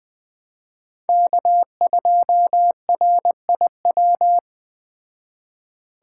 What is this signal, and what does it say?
Morse code, heard as a keyed tone, about 20 wpm, 700 Hz. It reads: K2RIW